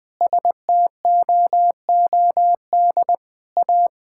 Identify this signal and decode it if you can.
Morse code, heard as a keyed tone, about 20 wpm, 700 Hz. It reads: STOOD A
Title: Sentences with 2-Word Phrases - 20wpm